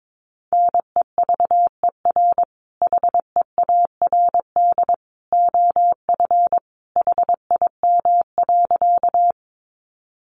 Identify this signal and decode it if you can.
Morse code, heard as a keyed tone, about 22 wpm, 700 Hz. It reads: NEVER HEARD OF HIM.
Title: Sentences from Top 400 Words 22wpm